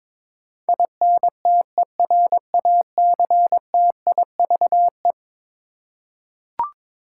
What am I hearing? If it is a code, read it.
Morse code, heard as a keyed tone, about 22 wpm, 700 Hz. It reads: INTERACTIVE
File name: practice / 22wpm / Eleven Letter Words - 22wpm